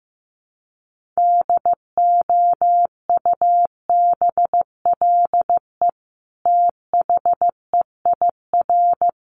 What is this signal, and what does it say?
Morse code, heard as a keyed tone, about 15 wpm, 700 Hz. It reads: DOUBLE THEIR
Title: Sentences from Top 1500 Words 15wpm